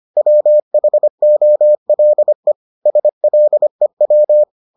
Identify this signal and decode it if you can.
Morse code, heard as a keyed tone, about 25 wpm, 600 Hz. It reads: WHOLE SLEW